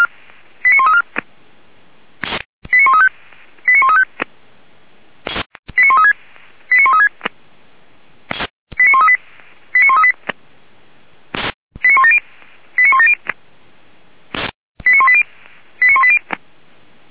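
Five-tone selective call (ZVEI), heard as quick runs of five tones, tones of about 80 ms. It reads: 89134 89135 89135 89136 89136 89138 89138 89170 89170 89180 89180
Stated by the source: ZVEI five-tone selective call received on SDR, Feuerwehr Orlamünde.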